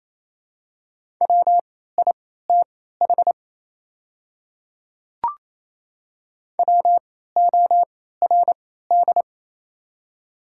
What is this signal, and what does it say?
Morse code, heard as a keyed tone, about 27 wpm, spaced as for 16 wpm, 690 Hz. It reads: WITH WORD
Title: 3x Character Spacing - Top 100 Words - 28wpm